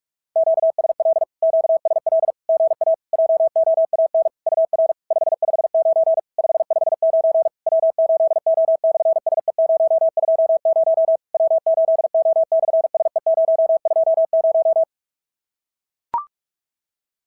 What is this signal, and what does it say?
Morse code, heard as a keyed tone, about 44 wpm, 650 Hz. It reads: QSL QSL GA JOAN UR 559 559 W8O/SE020 W8O/SE020
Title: SOTA QSO Practice 45wpm